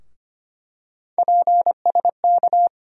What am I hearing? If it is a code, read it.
Morse code, heard as a keyed tone, about 25 wpm, 700 Hz. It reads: PSK
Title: USA Call Sign Postfixes - 3 Letters 25wpm